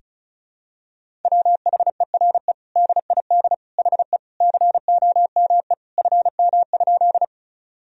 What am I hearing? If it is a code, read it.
Morse code, heard as a keyed tone, about 34 wpm, 690 Hz. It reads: WHERE DID HE COME FM?
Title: Sentences from Top 200 Words and Abbreviations 35wpm